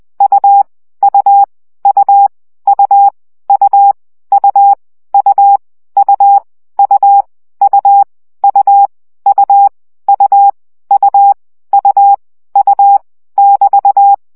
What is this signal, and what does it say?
Morse code, heard as a keyed tone, about 20 wpm, 800 Hz. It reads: U U U U U U U U U U U U U U U U <BT>
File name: CW gyakorlas u 17 baud